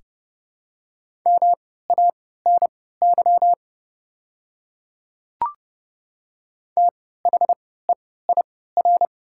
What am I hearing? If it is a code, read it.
Morse code, heard as a keyed tone, about 29 wpm, spaced as for 17 wpm, 700 Hz. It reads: MANY THEIR
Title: 3x Character Spacing - Top 100 Words - 30wpm